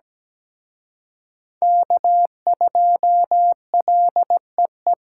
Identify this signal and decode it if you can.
Morse code, heard as a keyed tone, about 20 wpm, 700 Hz. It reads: K2LEE
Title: Speed Racing - USA 1x3 Call Signs 17wpm